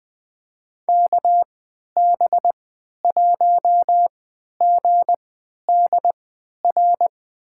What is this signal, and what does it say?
Morse code, heard as a keyed tone, about 20 wpm, spaced as for 11 wpm, 700 Hz. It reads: KB1GDR
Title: USA Call Signs - 6 Characters - 3x Character Spacing - 20wpm